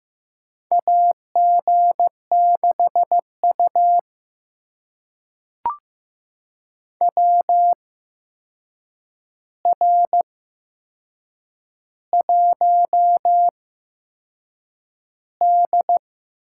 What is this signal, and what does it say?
Morse code, heard as a keyed tone, about 15 wpm, spaced as for 6 wpm, 700 Hz. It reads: AG6U WR1D
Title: USA Call Signs - 4 Characters - 8x Character Spacing - 15wpm